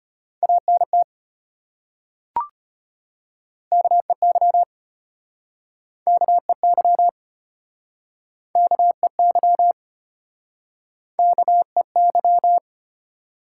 Morse code, heard as a keyed tone, about 34 wpm, 700 Hz. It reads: ANT KEY KEY KEY KEY